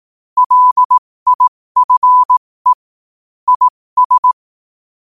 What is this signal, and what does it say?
Morse code, heard as a keyed tone, about 18 wpm, spaced as for 15 wpm, 1000 Hz. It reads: LIFE IS
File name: qotdfiles / 15wpm